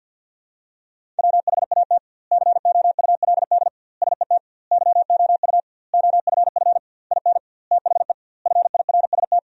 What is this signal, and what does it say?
Morse code, heard as a keyed tone, about 50 wpm, 700 Hz. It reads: WHAT COULD SET YOU OFF IN THE FIRST
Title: 2x Word-Spacing - Sentences from Top 200 Words - 50wpm